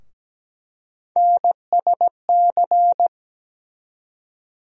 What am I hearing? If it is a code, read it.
Morse code, heard as a keyed tone, about 17 wpm, 700 Hz. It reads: NSC
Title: USA Call Sign Postfixes - 3 Letters 17wpm